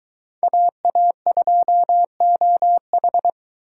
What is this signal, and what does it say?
Morse code, heard as a keyed tone, about 22 wpm, 700 Hz. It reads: AA2OH